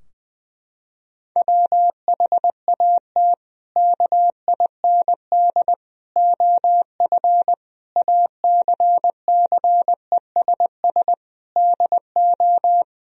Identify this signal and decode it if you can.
Morse code, heard as a keyed tone, about 20 wpm, 700 Hz. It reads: WHAT KIND OF ACCESS DO
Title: Sentences from Top 1000 Words - Review Part 4 - 20wpm